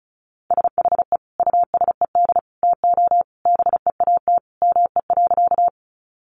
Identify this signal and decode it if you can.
Morse code, heard as a keyed tone, about 34 wpm, 700 Hz. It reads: SHE USED TO BEAT ME.
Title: Sentences from Top 1300 Words 35wpm